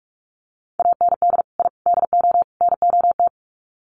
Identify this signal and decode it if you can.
Morse code, heard as a keyed tone, about 44 wpm, 700 Hz. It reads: AND I DO NOT
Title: Sets of 4 Words 45wpm